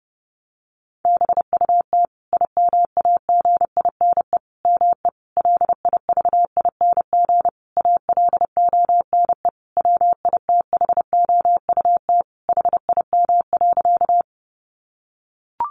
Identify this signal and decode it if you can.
Morse code, heard as a keyed tone, about 30 wpm, 700 Hz. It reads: BUT IMAGINE ME LIVING ALONE WITHOUT HIM.
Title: Sentences from Top 1400 Words 30wpm